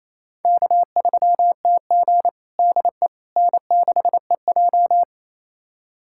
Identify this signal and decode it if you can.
Morse code, heard as a keyed tone, about 28 wpm, 700 Hz. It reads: K3TG DE N6EJ
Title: USA Callsign DE Callsign - 4 Characters - 28wpm